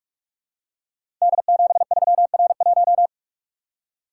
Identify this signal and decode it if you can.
Morse code, heard as a keyed tone, about 44 wpm, 700 Hz. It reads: D73R1
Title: Code Groups - 5 Letter:Numbers - 45wpm